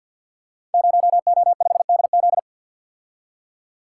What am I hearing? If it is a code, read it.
Morse code, heard as a keyed tone, about 51 wpm, 700 Hz. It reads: 0O5DZ
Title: Code Groups - 5 Letter:Numbers - 50wpm